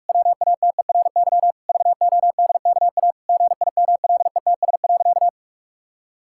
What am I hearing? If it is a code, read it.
Morse code, heard as a keyed tone, about 46 wpm, 700 Hz. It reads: WATERY VODKA GIMLETS.
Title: Pangram Challenge v2 - 45wpm